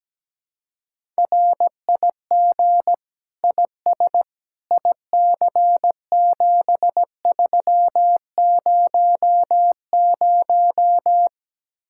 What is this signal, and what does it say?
Morse code, heard as a keyed tone, about 17 wpm, 700 Hz. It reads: RIG IS IC7300
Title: Speed-Racing Common QSO Phrases - 17wpm